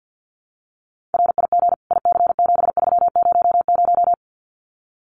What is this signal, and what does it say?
Morse code, heard as a keyed tone, about 50 wpm, 700 Hz. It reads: RIG IC7300
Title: Common QSO Phrases Encore 50wpm